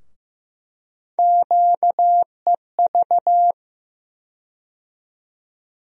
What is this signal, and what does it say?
Morse code, heard as a keyed tone, about 16 wpm, 700 Hz. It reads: QEV
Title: USA Call Sign Postfixes - 3 Letters 15wpm